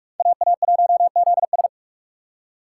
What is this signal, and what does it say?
Morse code, heard as a keyed tone, about 58 wpm, 700 Hz. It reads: AA1ZS